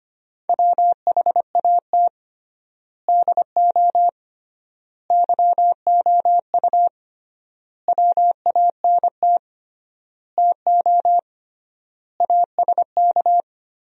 Morse code, heard as a keyed tone, about 25 wpm, 700 Hz. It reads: WHAT DO YOU WANT TO ASK
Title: 3x Word-Spacing - Sentences from Top 200 Words - 25wpm